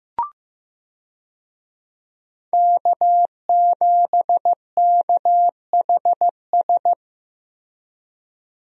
Morse code, heard as a keyed tone, about 15 wpm, 700 Hz. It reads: K7KHS
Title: USA Call Signs - 5 Characters - 15wpm